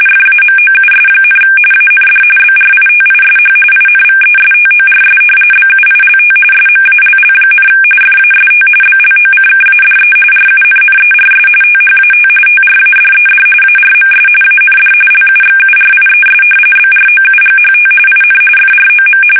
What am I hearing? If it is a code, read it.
RTTY, about 45.45 baud. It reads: FAMOUS, AND BRITISH INVESTORS BUILT SMALL, FAST BLOCKADE RUNNERS THAT TRADED ARMS AND LUXURIES BROUGHT IN FROM
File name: fsk850 / usnavy3